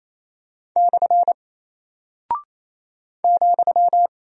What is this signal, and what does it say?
Morse code, heard as a keyed tone, about 28 wpm, 700 Hz. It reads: / ,
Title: Single Character (Letter:Number:Punctuation:Prosign) Alt 28wpm